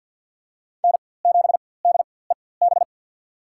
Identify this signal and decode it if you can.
Morse code, heard as a keyed tone, about 47 wpm, spaced as for 23 wpm, 700 Hz. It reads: N7DEB
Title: USA Call Signs - 5 Characters - 4x Character Spacing - 50wpm